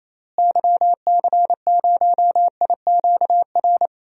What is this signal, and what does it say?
Morse code, heard as a keyed tone, about 28 wpm, 700 Hz. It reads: YC0IQR
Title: DXCC-100 Call Signs - 6 Characters - Repeated 2x - 28wpm